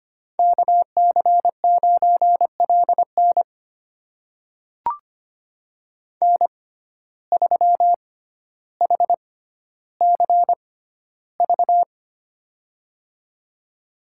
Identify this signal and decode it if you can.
Morse code, heard as a keyed tone, about 25 wpm, spaced as for 13 wpm, 690 Hz. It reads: KC9LN N3HCV
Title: USA Call Signs - 5 Characters - 6x Character Spacing - 25wpm